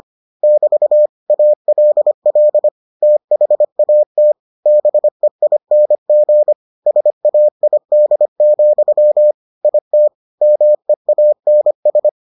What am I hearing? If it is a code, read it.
Morse code, heard as a keyed tone, about 25 wpm, 600 Hz. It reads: <BT> ALL THAT BEING SAID, IT MEANS